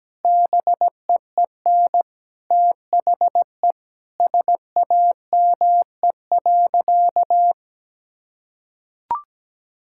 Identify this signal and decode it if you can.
Morse code, heard as a keyed tone, about 17 wpm, 700 Hz. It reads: BEEN THE SAME.
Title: Sentences from Top 1000 Words - Review Part 6 - 17wpm